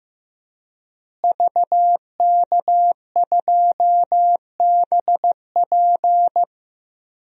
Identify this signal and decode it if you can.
Morse code, heard as a keyed tone, about 15 wpm, 700 Hz. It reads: VK2BP
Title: DXCC-100 Call Signs - 5 Characters - Repeated 2x - 15wpm